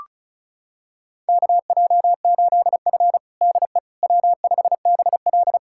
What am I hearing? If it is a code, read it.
Morse code, heard as a keyed tone, about 34 wpm, 700 Hz. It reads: KJ8F DE W5BL
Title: USA Callsign DE Callsign - 4 Characters - 35wpm